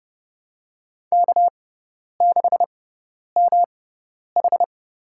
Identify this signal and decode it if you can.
Morse code, heard as a keyed tone, about 30 wpm, spaced as for 10 wpm, 700 Hz. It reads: K6MH
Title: USA Call Signs - 6x Character Spacing - 30wpm